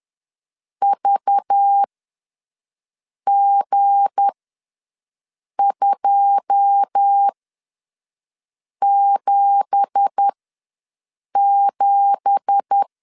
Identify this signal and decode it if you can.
Morse code, heard as a keyed tone, about 10 wpm, spaced as for 5 wpm, 800 Hz. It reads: VG277
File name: LEZIONI / 9 2867 ws 10wpm 4215